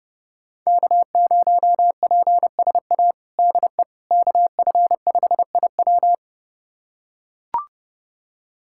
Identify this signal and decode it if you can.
Morse code, heard as a keyed tone, about 31 wpm, 700 Hz. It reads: K0PSA DE KF5IW
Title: USA Callsign DE Callsign - 5 Characters - 30wpm